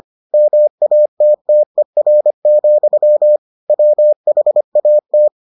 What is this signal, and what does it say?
Morse code, heard as a keyed tone, about 25 wpm, 600 Hz. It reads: MATTER, WHAT